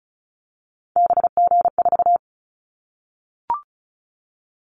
Morse code, heard as a keyed tone, about 34 wpm, 700 Hz. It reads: BG4
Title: Callsign Prefixes for Top 100 DXCC - 35wpm